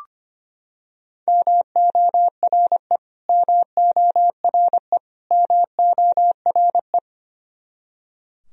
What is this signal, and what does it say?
Morse code, heard as a keyed tone, about 25 wpm, 700 Hz. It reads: MORE MORE MORE
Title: Top 100 Words Repeated 25wpm